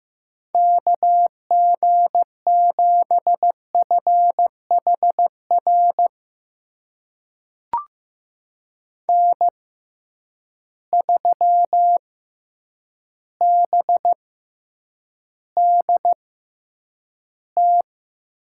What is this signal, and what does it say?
Morse code, heard as a keyed tone, about 15 wpm, spaced as for 8 wpm, 700 Hz. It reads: KG7FHR N3BDT
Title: USA Call Signs - 6x Character Spacing - 15wpm